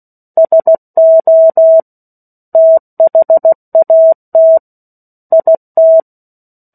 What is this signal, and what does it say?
Morse code, heard as a keyed tone, about 16 wpm, 650 Hz. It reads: SO THAT IT
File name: 03-Chapter-03 16 WPM